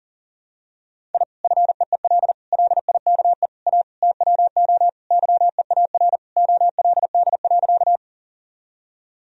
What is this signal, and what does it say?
Morse code, heard as a keyed tone, about 40 wpm, 700 Hz. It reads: I FEEL LIKE A TWO YEAR OLD.
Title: Sentences from Top 1000 Words - Review Part 1 - 40wpm